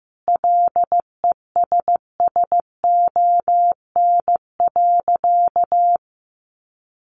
Morse code, heard as a keyed tone, about 15 wpm, 700 Hz. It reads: LESSON.
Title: Sentences from Top 1200 Words 15wpm